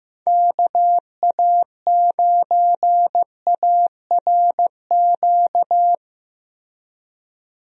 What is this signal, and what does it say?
Morse code, heard as a keyed tone, about 15 wpm, 700 Hz. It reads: KA9ARQ
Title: USA Call Signs - 6 Characters - Repeated 2x - 15wpm